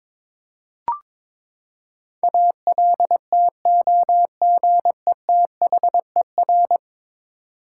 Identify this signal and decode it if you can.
Morse code, heard as a keyed tone, about 23 wpm, 700 Hz. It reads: ALTOGETHER
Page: morse-code-ninja-practice-22wpm